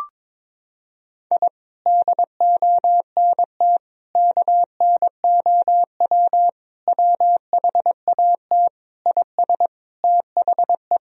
Morse code, heard as a keyed tone, about 22 wpm, 700 Hz. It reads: I DONT KNOW WHAT IS THE
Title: Sentences from Top 1000 Words - Review Part 4 - 22wpm